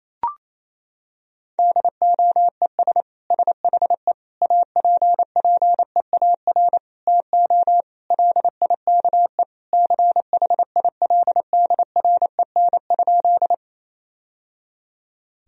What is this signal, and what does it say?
Morse code, heard as a keyed tone, about 27 wpm, 700 Hz. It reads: DOES SHE APPEAR TO LIKE CHILDREN?
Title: Sentences from Top 1000 Words - Review Part 2 - 28wpm